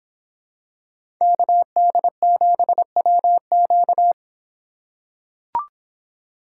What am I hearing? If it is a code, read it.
Morse code, heard as a keyed tone, about 26 wpm, 700 Hz. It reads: KD7WQ